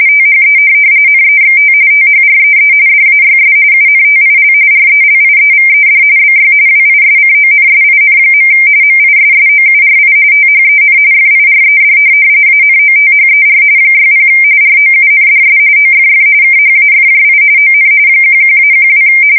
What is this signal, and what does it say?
RTTY, about 45.45 baud. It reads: THE NEXT THREE DAYS UNTIL RELIEVED BY WISCONSIN. MISSOURI THEN FIRED ANOTHER 60 ROUNDS OFF KHAFJI ON 11-12
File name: fsk170 / USS-Missouri